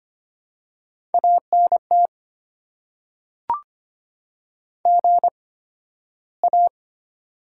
Morse code, heard as a keyed tone, about 25 wpm, spaced as for 13 wpm, 690 Hz. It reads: ANT GA
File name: Top 90 QSO Elements - 8x Character Spacing - 25wpm